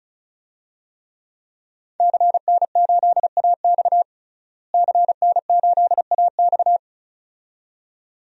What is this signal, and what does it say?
Morse code, heard as a keyed tone, about 34 wpm, 700 Hz. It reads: CN8AX CN8AX
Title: DXCC-100 Call Signs - 5 Characters - Repeated 2x - 35wpm